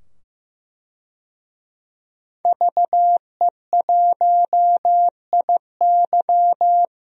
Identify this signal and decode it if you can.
Morse code, heard as a keyed tone, about 15 wpm, 700 Hz. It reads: VE1IY
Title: DXCC-100 Call Signs - 5 Characters - 15wpm